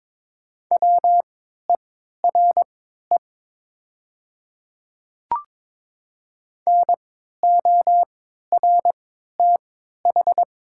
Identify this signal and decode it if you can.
Morse code, heard as a keyed tone, about 23 wpm, spaced as for 13 wpm, 700 Hz. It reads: WERE NORTH
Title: 3x Character Spacing - Top 300 Words - 22wpm